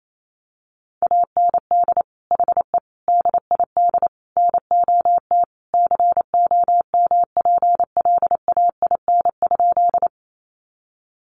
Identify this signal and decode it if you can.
Morse code, heard as a keyed tone, about 28 wpm, 700 Hz. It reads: AND HE DID NOT COMPLAIN?
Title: Sentences from Top 1300 Words 28wpm